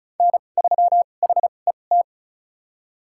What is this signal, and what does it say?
Morse code, heard as a keyed tone, about 36 wpm, spaced as for 25 wpm, 700 Hz. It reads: N3HET
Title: USA Call Signs - 5 Characters - 2x Character Spacing - 35wpm